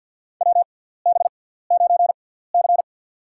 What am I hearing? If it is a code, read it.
Morse code, heard as a keyed tone, about 48 wpm, spaced as for 17 wpm, 700 Hz. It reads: WB9C
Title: USA Call Signs - 6x Character Spacing - 50wpm